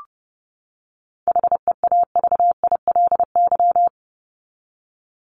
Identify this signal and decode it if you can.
Morse code, heard as a keyed tone, about 30 wpm, 700 Hz. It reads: HEAVILY
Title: LY Suffix - Top 100 Words with Base Words 30wpm